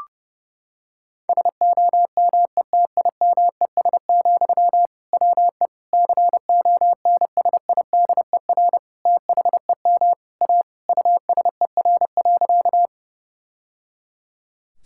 Morse code, heard as a keyed tone, about 30 wpm, 700 Hz. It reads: SOMETIMES, WE CONSIDER THEM A USER.
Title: Sentences from Top 1000 Words - Review Part 5 - 30wpm